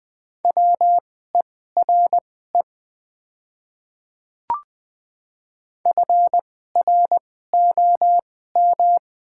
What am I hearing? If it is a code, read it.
Morse code, heard as a keyed tone, about 20 wpm, spaced as for 14 wpm, 690 Hz. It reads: WERE FROM
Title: 2x Character Spacing - Top 300 Words - 20wpm